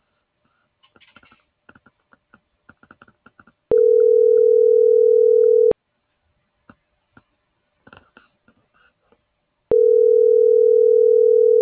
Ringback tone (North America), 440 + 480 Hz.